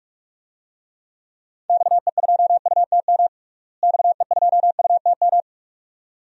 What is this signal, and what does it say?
Morse code, heard as a keyed tone, about 46 wpm, 700 Hz. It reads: XE2UTM XE2UTM
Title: DXCC-100 Call Signs - 6 Characters - Repeated 2x - 45wpm